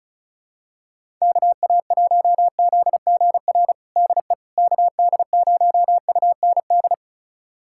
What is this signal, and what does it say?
Morse code, heard as a keyed tone, about 34 wpm, 700 Hz. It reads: KA1ZGR DE KD0UND
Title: USA Callsign DE Callsign - 6 Characters - 35wpm